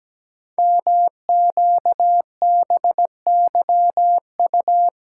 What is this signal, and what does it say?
Morse code, heard as a keyed tone, about 17 wpm, 700 Hz. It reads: MQBYU
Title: Code Groups - 5 Letters - 17wpm